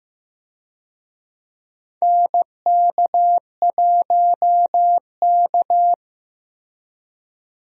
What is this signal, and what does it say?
Morse code, heard as a keyed tone, about 15 wpm, 700 Hz. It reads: NK1K
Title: USA Call Signs - 4 Characters - Repeated 2x - 15wpm